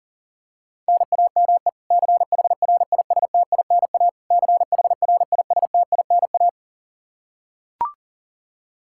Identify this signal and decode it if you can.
Morse code, heard as a keyed tone, about 37 wpm, 700 Hz. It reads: NAME CHRISTINA CHRISTINA
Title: K1USN SST QSO Practice 40wpm